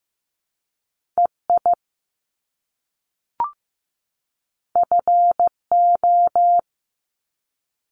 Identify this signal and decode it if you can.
Morse code, heard as a keyed tone, about 15 wpm, 700 Hz. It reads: EI FO
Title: Top Two Letter Combinations in Words 15wpm